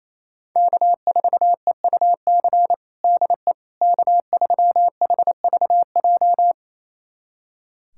Morse code, heard as a keyed tone, about 28 wpm, 700 Hz. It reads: K4EUC DE K3HVJ
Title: USA Callsign DE Callsign - 5 Characters - 28wpm